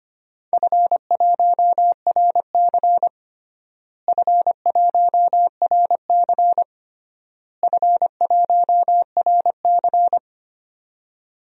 Morse code, heard as a keyed tone, about 25 wpm, 700 Hz. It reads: F1RC F1RC F1RC
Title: DXCC-100 Call Signs - 4 Characters - Repeated 3x - 25wpm